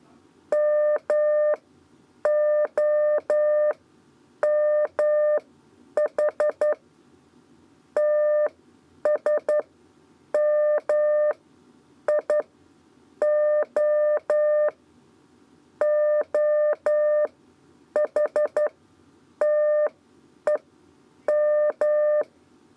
Morse code, heard as a keyed tone, about 11 wpm, spaced as for 8 wpm, 600 Hz. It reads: MOMH TSMIO OHTEM
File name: Lezione trasmessa 2 lez 2 exe